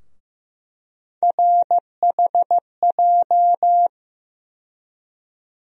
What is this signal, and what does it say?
Morse code, heard as a keyed tone, about 15 wpm, 700 Hz. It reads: RHJ
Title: USA Call Sign Postfixes - 3 Letters 15wpm